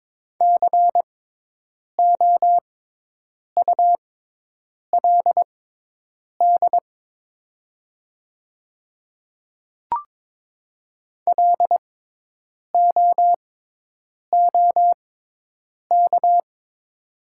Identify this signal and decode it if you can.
Morse code, heard as a keyed tone, about 22 wpm, spaced as for 8 wpm, 700 Hz. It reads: COULD LOOK
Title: 6x Character Spacing - Top 200 Words - 22wpm